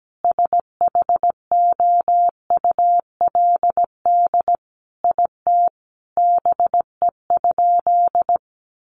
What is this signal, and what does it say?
Morse code, heard as a keyed tone, about 17 wpm, 700 Hz. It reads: SHOULD IT BE?
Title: Sentences from Top 300 Words 17wpm